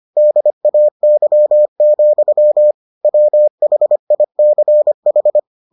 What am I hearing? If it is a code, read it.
Morse code, heard as a keyed tone, about 25 wpm, 600 Hz. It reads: DAY, WHICH